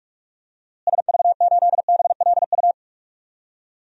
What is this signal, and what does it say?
Morse code, heard as a keyed tone, about 46 wpm, 700 Hz. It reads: SV8BLU
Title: DXCC-100 Call Signs - 6 Characters - Repeated 2x - 45wpm